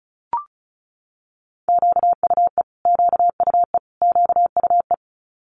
Morse code, heard as a keyed tone, about 36 wpm, 700 Hz. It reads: QUE QUE QUE
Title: Multi-letter Phonemes Repeated 3x 35wpm